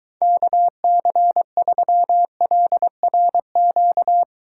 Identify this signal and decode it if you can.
Morse code, heard as a keyed tone, about 23 wpm, 700 Hz. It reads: KC3LRQ